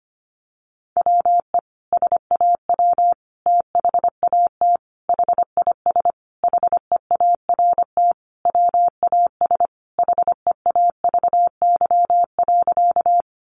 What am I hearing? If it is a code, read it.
Morse code, heard as a keyed tone, about 25 wpm, 700 Hz. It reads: WE SAW THAT HIS HEART WAS HEAVY.
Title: Sentences from Top 1200 Words 25wpm